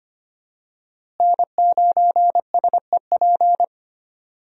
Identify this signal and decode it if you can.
Morse code, heard as a keyed tone, about 25 wpm, 700 Hz. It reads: N9SEP
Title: USA Call Signs - 5 Characters - Repeated 2x - 25wpm